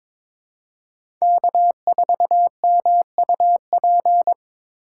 Morse code, heard as a keyed tone, about 22 wpm, 700 Hz. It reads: K4MUP
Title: USA Call Signs - 5 Characters - 4x Character Spacing - 22wpm